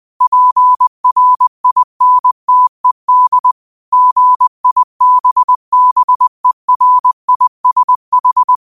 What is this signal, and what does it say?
Morse code, heard as a keyed tone, about 20 wpm, 1000 Hz. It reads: PRINTED GIBBERISH